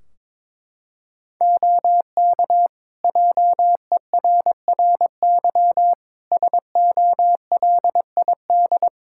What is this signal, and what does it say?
Morse code, heard as a keyed tone, about 22 wpm, 700 Hz. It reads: OK JERRY SOLID
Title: Practice QSO - Exchange 3 - More Info - 22wpm